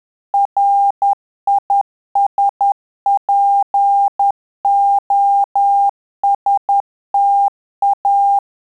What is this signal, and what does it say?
Morse code, heard as a keyed tone, about 10 wpm, 800 Hz. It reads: RISPOSTA
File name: Telegraphy-group / QSO CW 10 PPM